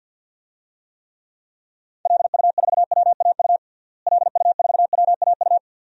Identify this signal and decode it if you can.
Morse code, heard as a keyed tone, about 52 wpm, 700 Hz. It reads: LU4WAU LU4WAU
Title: DXCC-100 Call Signs - 6 Characters - Repeated 3x - 50wpm